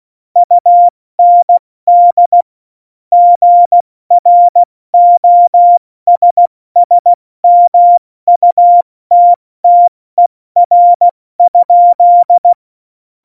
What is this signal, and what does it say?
Morse code, heard as a keyed tone, about 16 wpm, 700 Hz. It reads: UND GROSSMUTTER?
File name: Morsekurs / Witze / Witz 19. Februar 2019, 70 F80